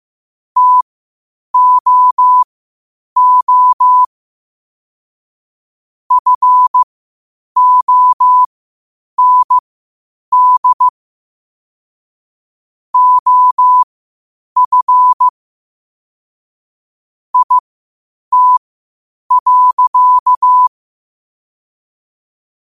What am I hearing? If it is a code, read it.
Morse code, heard as a keyed tone, about 15 wpm, spaced as for 8 wpm, 1000 Hz. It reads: TOO FOND OF IT.
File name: qotdfiles / 8wpm